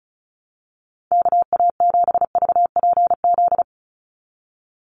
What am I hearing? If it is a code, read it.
Morse code, heard as a keyed tone, about 34 wpm, 700 Hz. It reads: KA7VPZ
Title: USA Call Signs 35wpm